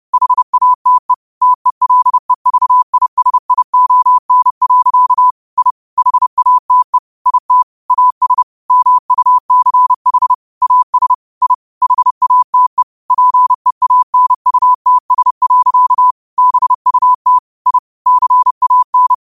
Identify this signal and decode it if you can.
Morse code, heard as a keyed tone, about 30 wpm, 1000 Hz. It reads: HATE TELEVISION. I HATE IT AS MUCH AS I HATE PEANUTS. BUT I CAN